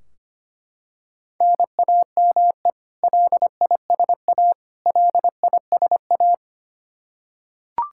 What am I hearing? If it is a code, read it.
Morse code, heard as a keyed tone, about 25 wpm, 700 Hz. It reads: NAME LISA LISA
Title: Practice QSO - Exchange 1 - Beginning - 25wpm